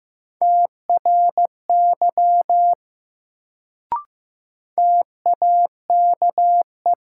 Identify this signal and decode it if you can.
Morse code, heard as a keyed tone, about 15 wpm, 700 Hz. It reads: TRY TAKE
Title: Sentences from Top 200 Words - New Word Drill - 15wpm